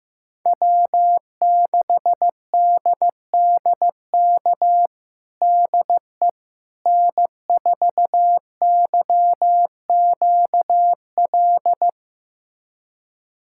Morse code, heard as a keyed tone, about 15 wpm, 700 Hz. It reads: W6DDK DE N4YQL
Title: USA Callsign DE Callsign - 5 Characters - 15wpm